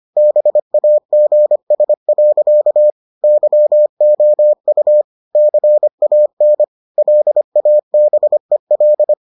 Morse code, heard as a keyed tone, about 25 wpm, 600 Hz. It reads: BAGS. YOU CAN LABEL